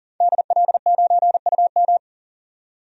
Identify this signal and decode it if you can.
Morse code, heard as a keyed tone, about 40 wpm, 700 Hz. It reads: DL9UM
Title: DXCC-100 Call Signs - 5 Characters - 40wpm